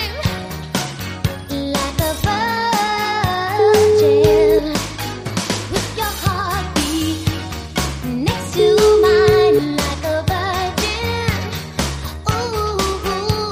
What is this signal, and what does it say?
Ringback tone (Europe, 425 Hz), with noise.